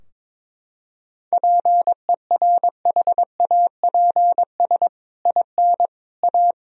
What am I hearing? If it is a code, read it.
Morse code, heard as a keyed tone, about 22 wpm, 700 Hz. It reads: PERHAPS IN A
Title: Sentences from Top 1000 Words - Review Part 3 - 22wpm